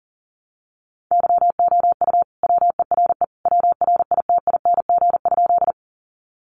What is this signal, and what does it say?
Morse code, heard as a keyed tone, about 40 wpm, 700 Hz. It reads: YOU WERE WRITING?
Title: Sentences from Top 800 Words 40wpm